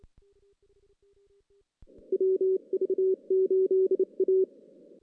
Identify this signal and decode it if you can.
Morse code, heard as a keyed tone, about 28 wpm, 380 Hz. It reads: WV8A
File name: 2016 JAN CW NAQP / 2016 Jan naqp k5ot - 0-5